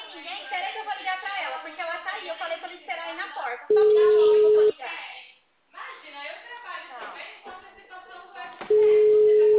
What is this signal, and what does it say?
Ringback tone (Europe, 425 Hz), with speech.